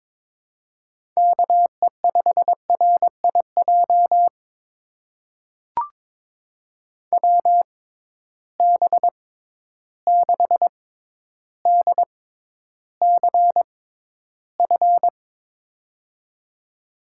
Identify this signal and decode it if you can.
Morse code, heard as a keyed tone, about 22 wpm, spaced as for 11 wpm, 700 Hz. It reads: KE5RIJ WB6DCF
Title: USA Call Signs - 6 Characters - 6x Character Spacing - 22wpm